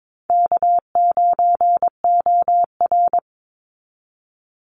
Morse code, heard as a keyed tone, about 22 wpm, 700 Hz. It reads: K9OR
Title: Easy Call Signs - 22wpm